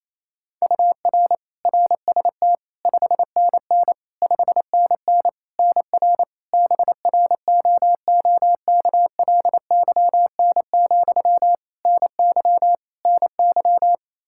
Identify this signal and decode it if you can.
Morse code, heard as a keyed tone, about 27 wpm, 700 Hz. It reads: UR RST 5NN 5NN NR BROOKLYN, NY NY
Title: K1USN SST QSO Practice 28wpm